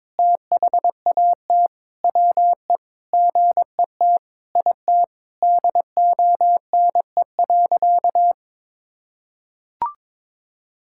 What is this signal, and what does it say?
Morse code, heard as a keyed tone, about 22 wpm, 700 Hz. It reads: THAT WE GET IT DONE.
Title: Sentences from Top 1600 Words 22wpm